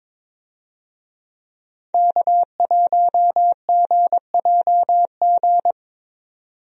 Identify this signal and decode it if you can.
Morse code, heard as a keyed tone, about 22 wpm, 700 Hz. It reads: K1GJG
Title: USA Call Signs - 5 Characters - 22wpm